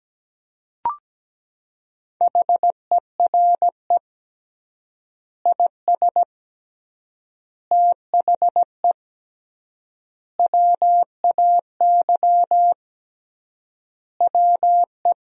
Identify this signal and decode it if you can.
Morse code, heard as a keyed tone, about 17 wpm, 700 Hz. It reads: HERE IS THE WAY WE
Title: 3x Word-Spacing - Sentences from Top 200 Words - 17wpm